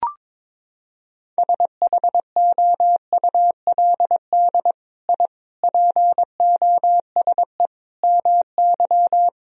Morse code, heard as a keyed tone, about 23 wpm, 700 Hz. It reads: SHOULD I POSE MY
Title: Sentences from Top 1000 Words - Review Part 2 - 22wpm